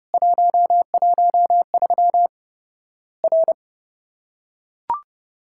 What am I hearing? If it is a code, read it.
Morse code, heard as a keyed tone, about 30 wpm, 700 Hz. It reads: 113 R
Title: Worked All Europe DX Contest - QTC Exchange - 30wpm